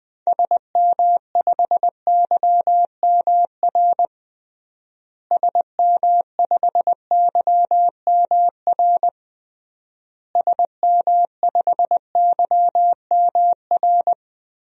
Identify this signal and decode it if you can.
Morse code, heard as a keyed tone, about 19 wpm, 700 Hz. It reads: SM5YMR SM5YMR SM5YMR
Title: DXCC-100 Call Signs - 6 Characters - Repeated 3x - 20wpm